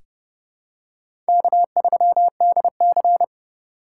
Morse code, heard as a keyed tone, about 30 wpm, 700 Hz. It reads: K3DC
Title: USA Call Signs - 4x Character Spacing - 30wpm